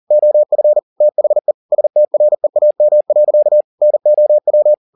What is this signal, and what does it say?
Morse code, heard as a keyed tone, about 40 wpm, 600 Hz. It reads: OF THE STREAM. NOW
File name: the old man and the sea 2 40wpm